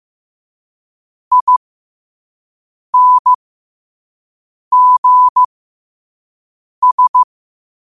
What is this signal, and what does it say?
Morse code, heard as a keyed tone, about 15 wpm, spaced as for 5 wpm, 1000 Hz. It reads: INGS
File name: qotdfiles / 5wpm